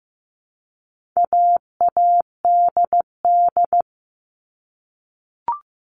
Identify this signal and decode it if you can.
Morse code, heard as a keyed tone, about 15 wpm, 700 Hz. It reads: AADD
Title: All Cut Numbers - 4-Digit 15wpm